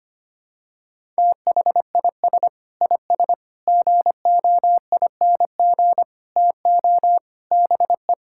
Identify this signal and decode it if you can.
Morse code, heard as a keyed tone, about 25 wpm, 700 Hz. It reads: THIS IS GOING TO BE
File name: practice / 25wpm / Sentences with 4-Word Phrases - 25wpm